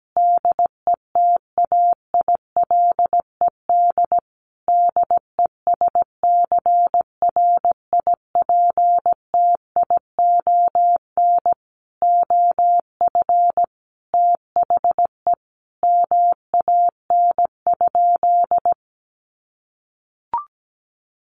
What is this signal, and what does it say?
Morse code, heard as a keyed tone, about 17 wpm, 700 Hz. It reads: DETAILED DESCRIPTION OF THE MAN?
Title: Sentences from Top 1500 Words 17wpm